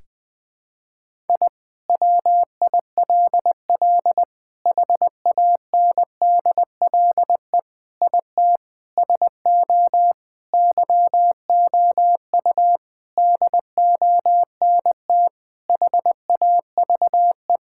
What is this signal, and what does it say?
Morse code, heard as a keyed tone, about 21 wpm, 700 Hz. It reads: I WILL HANDLE IT SO YOU DONT HAVE
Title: Sentences with 7-Word Phrases - Spoken Beginning - 20wpm